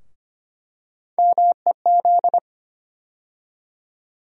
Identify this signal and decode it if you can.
Morse code, heard as a keyed tone, about 25 wpm, 700 Hz. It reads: MEZ
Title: USA Call Sign Postfixes - 3 Letters 25wpm